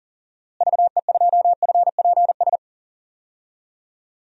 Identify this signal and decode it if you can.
Morse code, heard as a keyed tone, about 38 wpm, 700 Hz. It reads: VE2FPS
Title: DXCC-100 Call Signs - 6 Characters - Repeated 3x - 40wpm